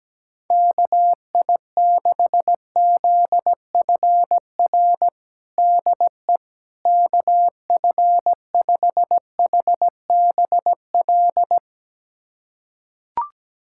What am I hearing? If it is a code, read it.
Morse code, heard as a keyed tone, about 17 wpm, 700 Hz. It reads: KI6ZFR DE KF5HBL
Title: USA Callsign DE Callsign - 6 Characters - 17wpm